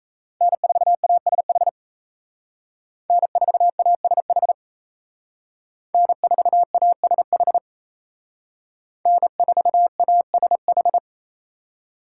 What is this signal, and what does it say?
Morse code, heard as a keyed tone, about 36 wpm, 700 Hz. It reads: N4ASH N4ASH N4ASH N4ASH